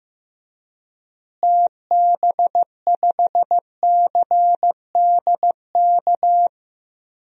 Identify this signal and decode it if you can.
Morse code, heard as a keyed tone, about 15 wpm, 700 Hz. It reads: TB5CDK
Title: DXCC-100 Call Signs - 6 Characters - Repeated 2x - 15wpm